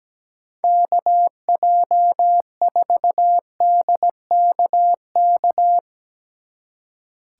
Morse code, heard as a keyed tone, about 17 wpm, 700 Hz. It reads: KJ4DKK
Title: Speed Racing - USA 2x3 Call Signs 17wpm